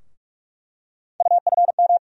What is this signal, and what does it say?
Morse code, heard as a keyed tone, about 46 wpm, 700 Hz. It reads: UFM
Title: USA Call Sign Postfixes - 3 Letters 45wpm